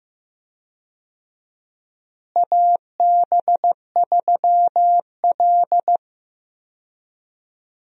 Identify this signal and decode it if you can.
Morse code, heard as a keyed tone, about 15 wpm, 700 Hz. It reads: AB3L
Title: USA Call Signs - 4 Characters - Repeated 2x - 15wpm